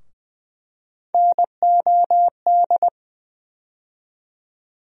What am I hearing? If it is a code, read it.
Morse code, heard as a keyed tone, about 20 wpm, 700 Hz. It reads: NOD
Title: USA Call Sign Postfixes - 3 Letters 20wpm